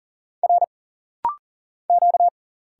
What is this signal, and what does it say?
Morse code, heard as a keyed tone, about 40 wpm, 700 Hz. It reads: R Q
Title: Single Letter-Number Alt - Rapid-Fire 40wpm